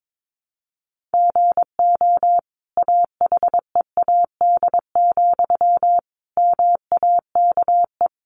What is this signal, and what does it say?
Morse code, heard as a keyed tone, about 22 wpm, 700 Hz. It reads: GO AHEAD, MAKE
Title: Sentences from Top 1100 Words 22wpm